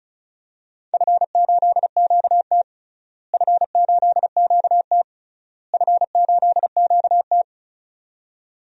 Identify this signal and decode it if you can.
Morse code, heard as a keyed tone, about 36 wpm, 700 Hz. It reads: F8QT F8QT F8QT
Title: DXCC-100 Call Signs - 4 Characters - Repeated 3x - 35wpm